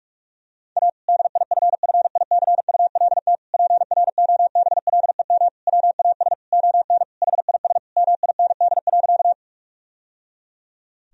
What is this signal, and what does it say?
Morse code, heard as a keyed tone, about 46 wpm, 700 Hz. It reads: A DIFFICULT PROBLEM WAS ON HIS MIND.
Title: Sentences from Top 1000 Words - Review Part 2 - 45wpm